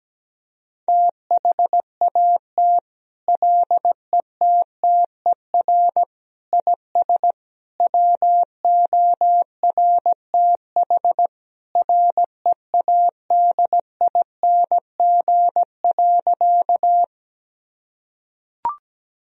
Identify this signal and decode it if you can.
Morse code, heard as a keyed tone, about 17 wpm, 700 Hz. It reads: THAT LETTER IS WORTH READING.
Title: Sentences from Top 1000 Words - Review Part 3 - 17wpm